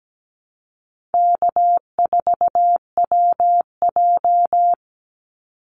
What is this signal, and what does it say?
Morse code, heard as a keyed tone, about 17 wpm, 700 Hz. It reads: K4WJ
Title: Easy Call Signs Repeated 3x - 17wpm